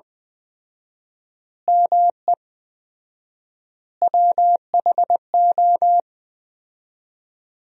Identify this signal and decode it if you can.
Morse code, heard as a keyed tone, about 20 wpm, 700 Hz. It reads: ME WHO
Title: 4x Word-Spacing - Sentences from Top 200 Words - 20wpm